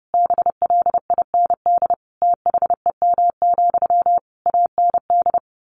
Morse code, heard as a keyed tone, about 30 wpm, 700 Hz. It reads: BLIND THEM, AND
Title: A Sound Of Thunder 30wpm